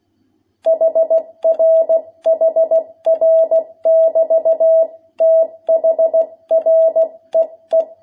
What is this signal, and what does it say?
Morse code, heard as a keyed tone, about 16 wpm, 660 Hz. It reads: HR HR <BT> THREE
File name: jh8cbh / 2021cw